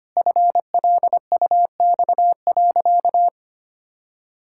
Morse code, heard as a keyed tone, about 25 wpm, 700 Hz. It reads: FLUX.
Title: Pangram Challenge v2 - 25wpm